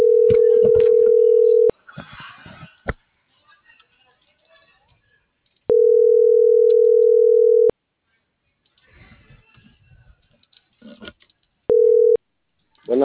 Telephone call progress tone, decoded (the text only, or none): ringback tone